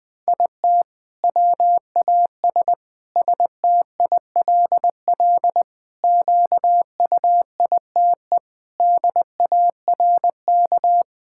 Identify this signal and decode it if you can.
Morse code, heard as a keyed tone, about 20 wpm, 700 Hz. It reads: IT WAS STILL QUITE DARK
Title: Sentences from Top 1000 Words - Review Part 6 - 20wpm